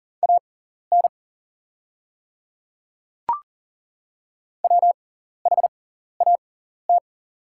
Morse code, heard as a keyed tone, about 40 wpm, spaced as for 14 wpm, 690 Hz. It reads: AN WHAT
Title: 6x Character Spacing - Top 100 Words - 40wpm